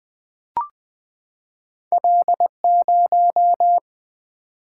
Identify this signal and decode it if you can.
Morse code, heard as a keyed tone, about 20 wpm, 700 Hz. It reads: L 0
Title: Single-Letter Single-Number 20wpm